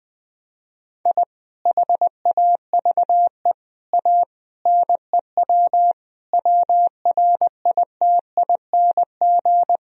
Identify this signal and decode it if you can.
Morse code, heard as a keyed tone, about 20 wpm, 700 Hz. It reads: I HAVE A NEW WRITING
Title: Sentences from Top 1000 Words - Review Part 3 - 20wpm